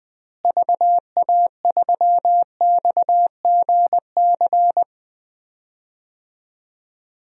Morse code, heard as a keyed tone, about 20 wpm, 700 Hz. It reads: VA3XGC
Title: DXCC-100 Call Signs - 6 Characters - Repeated 3x - 20wpm